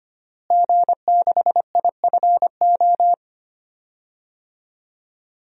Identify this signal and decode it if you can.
Morse code, heard as a keyed tone, about 25 wpm, 700 Hz. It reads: G6IFO
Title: DXCC-100 Call Signs - 5 Characters - Repeated 3x - 25wpm